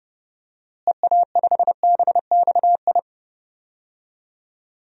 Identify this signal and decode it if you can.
Morse code, heard as a keyed tone, about 30 wpm, 700 Hz. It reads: EA5BXI
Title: DXCC-100 Call Signs - 6 Characters - Repeated 3x - 30wpm